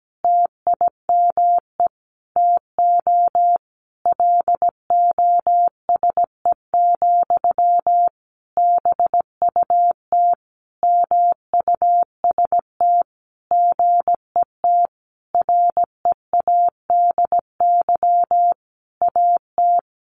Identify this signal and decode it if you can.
Morse code, heard as a keyed tone, about 17 wpm, 700 Hz. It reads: TIME TO LOSE, BUT MUST GET READY AT
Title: Sentences from Top 1100 Words 17wpm